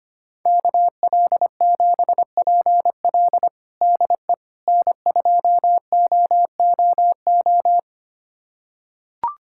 Morse code, heard as a keyed tone, about 25 wpm, 700 Hz. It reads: KL7PL DE N2OOO
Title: USA Callsign DE Callsign - 5 Characters - 25wpm